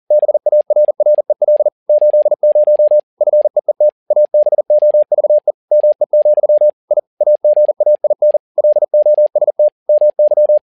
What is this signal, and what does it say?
Morse code, heard as a keyed tone, about 40 wpm, 600 Hz. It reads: BARREL 80 FEET ABOVE ME, I AGAIN LOST MY
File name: learning center / tower trouble / tower touble 40wpm